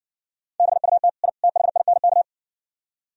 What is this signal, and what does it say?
Morse code, heard as a keyed tone, about 69 wpm, 700 Hz. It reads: BUT I THINK